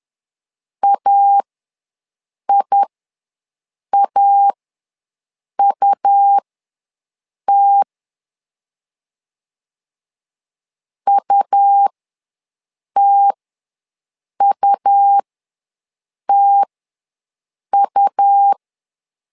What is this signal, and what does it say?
Morse code, heard as a keyed tone, about 11 wpm, spaced as for 6 wpm, 800 Hz. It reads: AIAUT UTUTU
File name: LEZIONI / 2 UHDN / 2 U ws 10wpm 2017